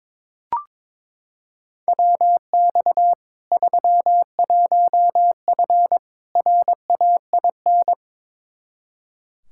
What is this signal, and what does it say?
Morse code, heard as a keyed tone, about 22 wpm, 700 Hz. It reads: WX 31F RAIN
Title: Practice QSO - Exchange 2 - Info - 22wpm